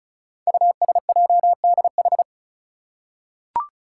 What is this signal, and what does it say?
Morse code, heard as a keyed tone, about 36 wpm, 690 Hz. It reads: USJDH
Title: Code Groups - 5 Letters - 35wpm